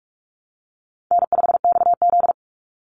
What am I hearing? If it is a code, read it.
Morse code, heard as a keyed tone, about 44 wpm, 700 Hz. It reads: N5XZ
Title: Easy Call Signs Repeated 3x - 45wpm